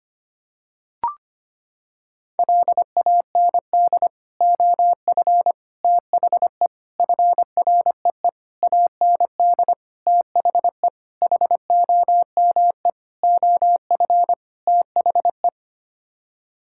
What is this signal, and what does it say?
Morse code, heard as a keyed tone, about 25 wpm, 700 Hz. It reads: LAND OF THE FREE AND THE HOME OF THE
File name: Sets of 9 Words - 25wpm